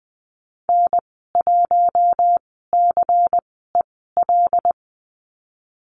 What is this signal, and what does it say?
Morse code, heard as a keyed tone, about 20 wpm, spaced as for 15 wpm, 700 Hz. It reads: N1CEL
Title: 179 Lesson 29g - Slash - Call Signs 20wpm